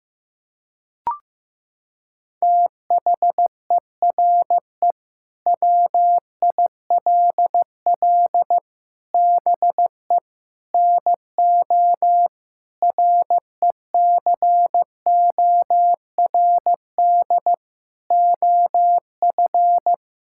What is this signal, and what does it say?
Morse code, heard as a keyed tone, about 15 wpm, 700 Hz. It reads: THERE WILL BE NO RECORD OF
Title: Sentences from Top 1000 Words - Review Part 5 - 15wpm